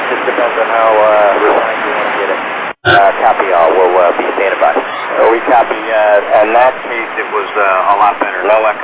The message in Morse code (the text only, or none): none